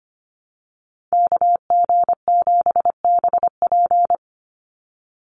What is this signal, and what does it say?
Morse code, heard as a keyed tone, about 25 wpm, 700 Hz. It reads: KG7BP
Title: USA Call Signs 25wpm